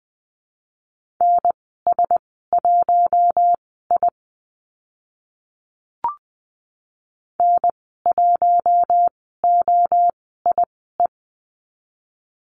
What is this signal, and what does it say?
Morse code, heard as a keyed tone, about 20 wpm, spaced as for 15 wpm, 700 Hz. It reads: NS1I N1OIE
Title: 042 Lesson 8g - Number 1 - Call Signs 20wpm